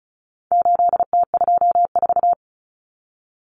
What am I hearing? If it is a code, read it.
Morse code, heard as a keyed tone, about 34 wpm, 700 Hz. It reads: 8T24
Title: Common Cut Numbers - 4-Digit 35wpm